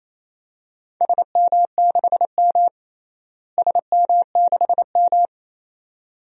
Morse code, heard as a keyed tone, about 28 wpm, 700 Hz. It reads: SM6M SM6M
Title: DXCC-100 Call Signs - 4 Characters - Repeated 2x - 28wpm